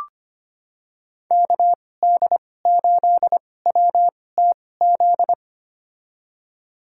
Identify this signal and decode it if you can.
Morse code, heard as a keyed tone, about 25 wpm, spaced as for 18 wpm, 700 Hz. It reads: KD8WTZ
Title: USA Call Signs - 2x Character Spacing - 25wpm